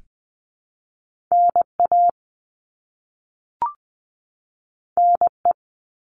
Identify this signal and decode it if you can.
Morse code, heard as a keyed tone, about 20 wpm, 700 Hz. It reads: N A N E
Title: All Cut Numbers - 2-Digit 20wpm